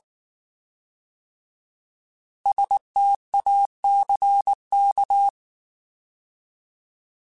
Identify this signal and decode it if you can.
Morse code, heard as a keyed tone, about 19 wpm, 800 Hz. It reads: STACK
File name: Alphabetical / Alphabetical mp3s / Ss @18